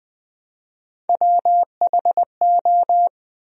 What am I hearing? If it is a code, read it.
Morse code, heard as a keyed tone, about 22 wpm, 700 Hz. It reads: WHO